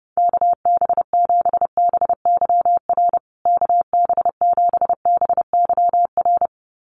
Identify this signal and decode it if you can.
Morse code, heard as a keyed tone, about 30 wpm, 700 Hz. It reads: KB7BYR KB7BYR
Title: USA Call Signs Repeated 2x 30wpm